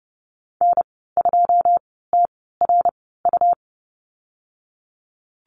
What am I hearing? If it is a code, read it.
Morse code, heard as a keyed tone, about 30 wpm, spaced as for 17 wpm, 700 Hz. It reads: N2TRU
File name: Lesson 19g - Letter w - Call Signs 30wpm